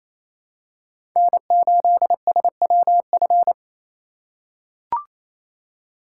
Morse code, heard as a keyed tone, about 28 wpm, 700 Hz. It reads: N8SWF
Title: USA Call Signs - 5 Characters - 4x Character Spacing - 28wpm